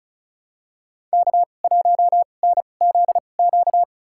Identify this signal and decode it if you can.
Morse code, heard as a keyed tone, about 34 wpm, spaced as for 25 wpm, 700 Hz. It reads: K1NZQ
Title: USA Call Signs - 5 Characters - 2x Character Spacing - 35wpm